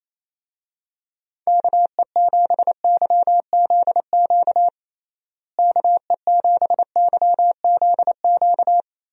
Morse code, heard as a keyed tone, about 28 wpm, 700 Hz. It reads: KE7YZQ KE7YZQ
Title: USA Call Signs - 6 Characters - Repeated 2x - 28wpm